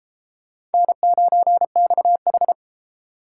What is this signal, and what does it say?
Morse code, heard as a keyed tone, about 31 wpm, 700 Hz. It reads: N9XH